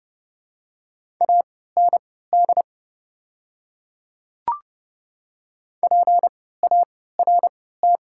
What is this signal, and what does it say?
Morse code, heard as a keyed tone, about 30 wpm, spaced as for 17 wpm, 700 Hz. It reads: AND PART
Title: 3x Character Spacing - Top 300 Words - 30wpm